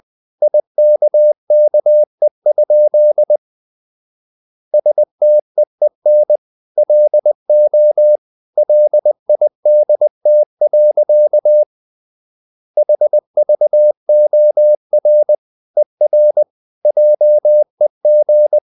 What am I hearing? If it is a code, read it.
Morse code, heard as a keyed tone, about 20 wpm, 600 Hz. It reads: IKKE? STEEN LO LIDT. HVOR ER JEG
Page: Ferie 20 wpm